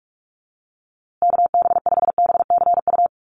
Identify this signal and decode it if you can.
Morse code, heard as a keyed tone, about 44 wpm, 700 Hz. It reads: KB5BCU
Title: USA Call Signs Repeated 2x 45wpm